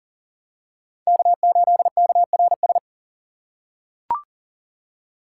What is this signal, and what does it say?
Morse code, heard as a keyed tone, about 40 wpm, 700 Hz. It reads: K8KRS
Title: USA Call Signs - 5 Characters - 6x Character Spacing - 40wpm